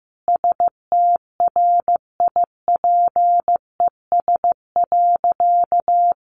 Morse code, heard as a keyed tone, about 15 wpm, 700 Hz. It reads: STRIPES.
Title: Additional Easy Phrases 15wpm